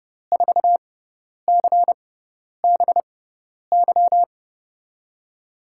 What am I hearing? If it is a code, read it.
Morse code, heard as a keyed tone, about 31 wpm, spaced as for 10 wpm, 700 Hz. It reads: 4CBY
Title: USA Call Signs - 6x Character Spacing - 30wpm